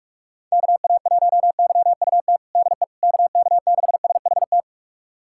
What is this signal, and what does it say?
Morse code, heard as a keyed tone, about 45 wpm, 700 Hz. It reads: KA1YUT DE KK6SHT
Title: USA Callsign DE Callsign 45wpm